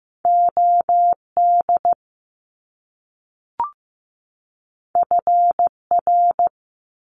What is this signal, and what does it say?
Morse code, heard as a keyed tone, about 15 wpm, 700 Hz. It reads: OD FR
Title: Top Two Letter Combinations in Words 15wpm